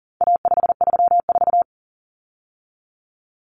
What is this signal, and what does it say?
Morse code, heard as a keyed tone, about 40 wpm, 700 Hz. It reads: A534
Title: Common Cut Numbers - 4-Digit 40wpm